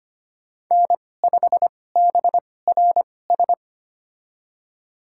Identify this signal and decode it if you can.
Morse code, heard as a keyed tone, about 25 wpm, spaced as for 18 wpm, 700 Hz. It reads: N5BRS
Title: USA Call Signs - 5 Characters - 2x Character Spacing - 25wpm